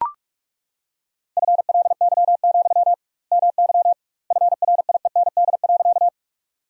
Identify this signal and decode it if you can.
Morse code, heard as a keyed tone, about 41 wpm, 700 Hz. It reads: FLY, MY FRIEND.
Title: 2x Word-Spacing - Sentences from Top 500 Words - 45wpm